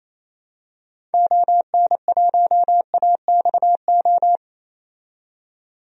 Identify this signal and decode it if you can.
Morse code, heard as a keyed tone, about 28 wpm, 700 Hz. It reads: ON1AXO
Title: DXCC-100 Call Signs - 6 Characters - 28wpm